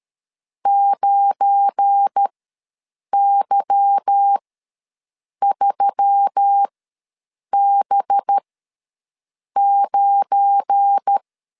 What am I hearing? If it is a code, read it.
Morse code, heard as a keyed tone, about 13 wpm, spaced as for 6 wpm, 800 Hz. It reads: 9Y3B9
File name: LEZIONI / 8 YJ45 / 8 Y ws 12wpm 3409